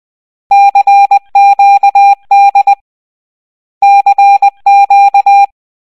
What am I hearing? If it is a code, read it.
Morse code, heard as a keyed tone, about 20 wpm, 800 Hz. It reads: CQD CQ
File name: raving-rabbids-travel-in-time-wii-gamerip-2010 / prmqdlitay / 9-06. Morse code